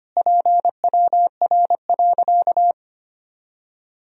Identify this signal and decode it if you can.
Morse code, heard as a keyed tone, about 25 wpm, 700 Hz. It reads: PWR.
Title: Sentences from Top 500 Words and Abbreviations 25wpm